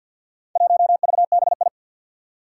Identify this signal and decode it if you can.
Morse code, heard as a keyed tone, about 50 wpm, 700 Hz. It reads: 1VBI
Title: Four Challenging Letter-Number Combinations 50wpm